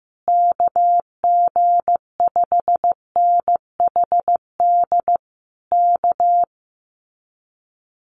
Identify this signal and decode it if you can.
Morse code, heard as a keyed tone, about 15 wpm, 700 Hz. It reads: KG5NHD K
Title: Common QSO Phrases Encore 15wpm